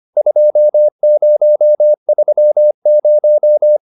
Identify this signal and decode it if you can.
Morse code, heard as a keyed tone, about 25 wpm, 600 Hz. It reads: 2030